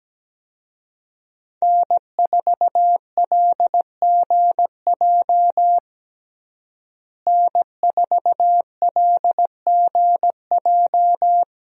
Morse code, heard as a keyed tone, about 17 wpm, 700 Hz. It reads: N4LGJ N4LGJ
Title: USA Call Signs - 5 Characters - Repeated 3x - 17wpm